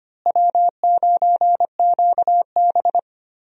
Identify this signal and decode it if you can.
Morse code, heard as a keyed tone, about 25 wpm, 700 Hz. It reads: W9QB
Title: USA Call Signs - 4 Characters - Repeated 2x - 25wpm